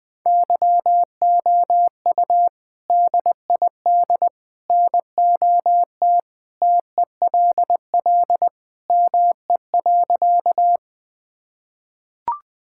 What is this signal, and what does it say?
Morse code, heard as a keyed tone, about 21 wpm, 700 Hz. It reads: YOU DID NOT TELL ME.
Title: Sentences with 3-Word Phrases - 20wpm